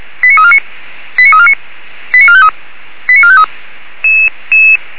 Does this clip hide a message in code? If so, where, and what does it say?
Yes: five-tone selective call (ZVEI) from 0.2 to 3.4 s, heard as quick runs of five tones, tones of about 70 ms. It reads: 89349 89349 89453 89453